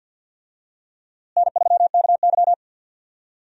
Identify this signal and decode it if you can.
Morse code, heard as a keyed tone, about 52 wpm, 700 Hz. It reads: N3KY
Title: USA Call Signs - 4 Characters - Repeated 2x - 50wpm